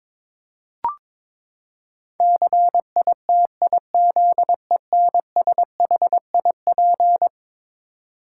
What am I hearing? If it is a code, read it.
Morse code, heard as a keyed tone, about 22 wpm, 700 Hz. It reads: CITIZENSHIP
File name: practice / 22wpm / Eleven Letter Words - 22wpm